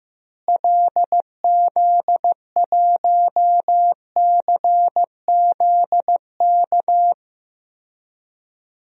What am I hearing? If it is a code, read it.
Morse code, heard as a keyed tone, about 15 wpm, 700 Hz. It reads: LZ1CZK
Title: DXCC-100 Call Signs - 6 Characters - Repeated 3x - 15wpm